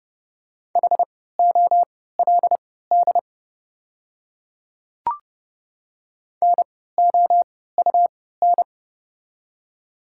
Morse code, heard as a keyed tone, about 30 wpm, spaced as for 17 wpm, 700 Hz. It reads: HOLD NOUN
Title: 3x Character Spacing - Top 500 Words - 30wpm